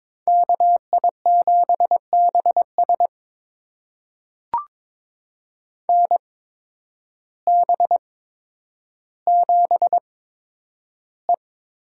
Morse code, heard as a keyed tone, about 22 wpm, spaced as for 10 wpm, 700 Hz. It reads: KI7BS NB7E
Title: USA Call Signs - 8x Character Spacing - 22wpm